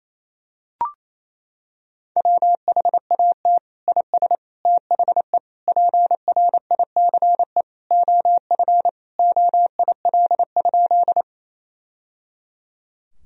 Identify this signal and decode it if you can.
Morse code, heard as a keyed tone, about 28 wpm, 700 Hz. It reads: WHAT IS THE PRICE OF OIL?
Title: Sentences from Top 1000 Words - Review Part 4 - 28wpm